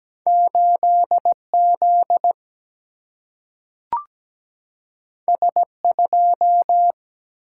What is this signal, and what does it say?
Morse code, heard as a keyed tone, about 17 wpm, 700 Hz. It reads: 8Z S2
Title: Single-Letter Single-Number 17wpm